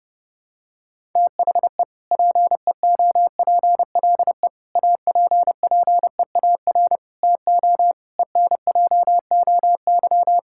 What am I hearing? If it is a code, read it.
Morse code, heard as a keyed tone, about 32 wpm, 700 Hz. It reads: THE PEOPLE APPEAR TO ENJOY
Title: Sentences from Top 1000 Words - Review Part 5 - 30wpm